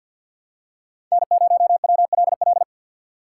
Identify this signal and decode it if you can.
Morse code, heard as a keyed tone, about 46 wpm, 700 Hz. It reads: N0WLL
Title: USA Call Signs - 2x Character Spacing - 50wpm